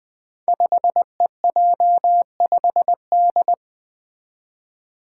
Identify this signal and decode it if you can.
Morse code, heard as a keyed tone, about 20 wpm, 700 Hz. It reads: 5EJ5D
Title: Five Challenging Letter-Number Combinations 20wpm